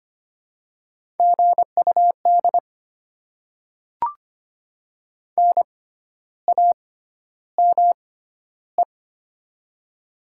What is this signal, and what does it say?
Morse code, heard as a keyed tone, about 25 wpm, spaced as for 12 wpm, 700 Hz. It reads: GUD NAME
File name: Top 20 QSO Elements - 6x Character Spacing - 25wpm